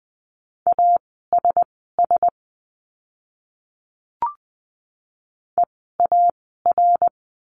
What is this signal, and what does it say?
Morse code, heard as a keyed tone, about 20 wpm, spaced as for 15 wpm, 700 Hz. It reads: ASS EAR
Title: 061 Lesson 11f - Letter h - Words 20wpm